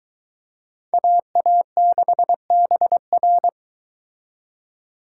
Morse code, heard as a keyed tone, about 22 wpm, 700 Hz. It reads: AA6BR